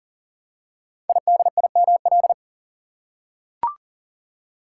Morse code, heard as a keyed tone, about 42 wpm, 700 Hz. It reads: IDIML
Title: Code Groups - 5 Letter:Numbers - 40wpm